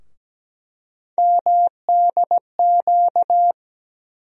Morse code, heard as a keyed tone, about 17 wpm, 700 Hz. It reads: MDQ
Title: USA Call Sign Postfixes - 3 Letters 17wpm